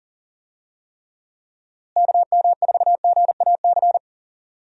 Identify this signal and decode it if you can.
Morse code, heard as a keyed tone, about 40 wpm, 700 Hz. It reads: KM4GAC
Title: USA Call Signs - 6 Characters - Repeated 3x - 40wpm